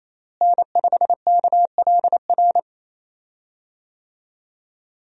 Morse code, heard as a keyed tone, about 28 wpm, 700 Hz. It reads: N5KLR
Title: USA Call Signs - 5 Characters - Repeated 3x - 28wpm